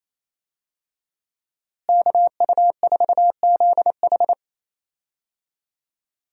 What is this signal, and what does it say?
Morse code, heard as a keyed tone, about 28 wpm, 700 Hz. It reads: KU4ZH
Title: USA Call Signs - 5 Characters - Repeated 3x - 28wpm